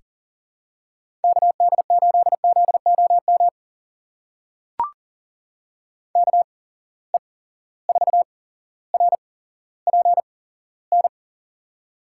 Morse code, heard as a keyed tone, about 40 wpm, spaced as for 17 wpm, 700 Hz. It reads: KD8ZOM KE4RPN
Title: USA Call Signs - 6 Characters - 8x Character Spacing - 40wpm